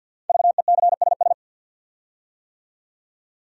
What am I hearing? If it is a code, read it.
Morse code, heard as a keyed tone, about 48 wpm, 700 Hz. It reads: VECSS
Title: Code Groups - 5 Letters - 50wpm